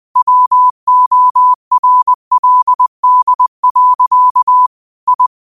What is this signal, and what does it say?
Morse code, heard as a keyed tone, about 20 wpm, 1000 Hz. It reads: WORLD. I
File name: 20wpm